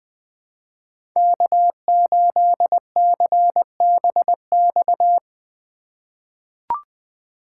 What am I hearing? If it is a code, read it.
Morse code, heard as a keyed tone, about 20 wpm, 700 Hz. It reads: K8CBX
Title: USA Call Signs - 4x Character Spacing - 20wpm